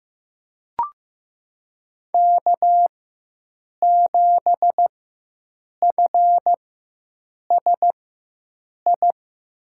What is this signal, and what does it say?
Morse code, heard as a keyed tone, about 15 wpm, spaced as for 7 wpm, 700 Hz. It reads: K7FSI
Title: USA Call Signs - 5 Characters - 4x Character Spacing - 15wpm